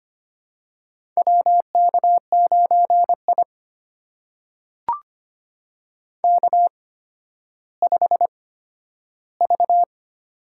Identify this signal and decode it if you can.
Morse code, heard as a keyed tone, about 25 wpm, spaced as for 12 wpm, 700 Hz. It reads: WK9I K5V
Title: USA Call Signs - 4 Characters - 8x Character Spacing - 25wpm